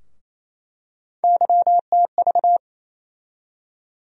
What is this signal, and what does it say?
Morse code, heard as a keyed tone, about 28 wpm, 700 Hz. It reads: YTV
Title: USA Call Sign Postfixes - 3 Letters 28wpm